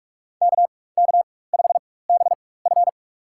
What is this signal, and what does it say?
Morse code, heard as a keyed tone, about 46 wpm, spaced as for 21 wpm, 700 Hz. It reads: KK5BF
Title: USA Call Signs - 5 Characters - 4x Character Spacing - 45wpm